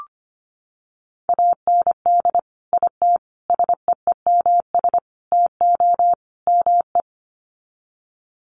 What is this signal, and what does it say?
Morse code, heard as a keyed tone, about 25 wpm, 700 Hz. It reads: AND IT SEEMS TO ME
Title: Sets of 5 Words Encore2 25wpm